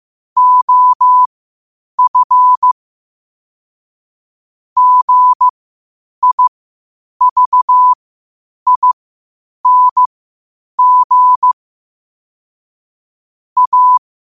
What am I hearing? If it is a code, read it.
Morse code, heard as a keyed tone, about 15 wpm, spaced as for 8 wpm, 1000 Hz. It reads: OF GIVING A
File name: qotdfiles / 8wpm